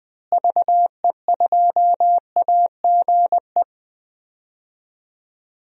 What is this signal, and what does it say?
Morse code, heard as a keyed tone, about 20 wpm, 700 Hz. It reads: VE2AGE
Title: DXCC-100 Call Signs - 6 Characters - Repeated 3x - 20wpm